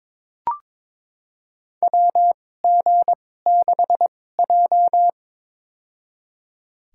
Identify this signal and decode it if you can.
Morse code, heard as a keyed tone, about 21 wpm, spaced as for 16 wpm, 700 Hz. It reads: WG6J
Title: USA Call Signs - 4 Characters - 2x Character Spacing - 22wpm